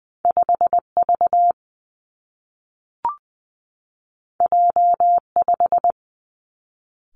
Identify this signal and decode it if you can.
Morse code, heard as a keyed tone, about 19 wpm, 700 Hz. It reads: 5 V J 5
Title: Two Challenging Letter-Number Combinations 20wpm